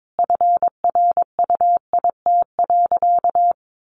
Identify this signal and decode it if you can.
Morse code, heard as a keyed tone, about 22 wpm, 700 Hz. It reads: FRUIT.
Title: Sentences from Top 1300 Words 22wpm